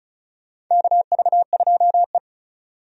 Morse code, heard as a keyed tone, about 34 wpm, 700 Hz. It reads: KV2E
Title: USA Call Signs - 4 Characters - 35wpm